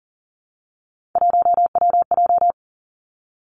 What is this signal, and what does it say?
Morse code, heard as a keyed tone, about 40 wpm, 700 Hz. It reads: 1WJ
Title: Three Challenging Letter-Number Combinations 40wpm